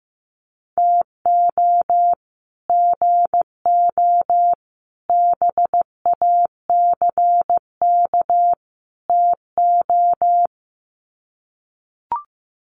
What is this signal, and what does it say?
Morse code, heard as a keyed tone, about 15 wpm, 700 Hz. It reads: TO GO BACK TO
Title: Sets of 4 Words 15wpm